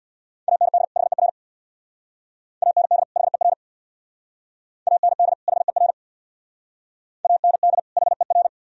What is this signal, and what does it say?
Morse code, heard as a keyed tone, about 64 wpm, 700 Hz. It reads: AND HER AND HER AND HER AND HER